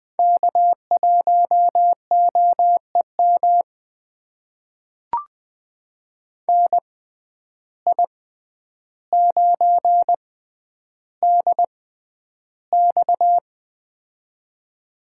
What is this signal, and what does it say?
Morse code, heard as a keyed tone, about 20 wpm, spaced as for 10 wpm, 700 Hz. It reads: K1OEM NI9DX
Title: USA Call Signs - 5 Characters - 6x Character Spacing - 20wpm